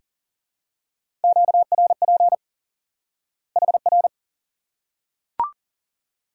Morse code, heard as a keyed tone, about 38 wpm, 700 Hz. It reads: QRP HR
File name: IQE Course 3 - Lesson 89 - YAGI - Two Elements - 40wpm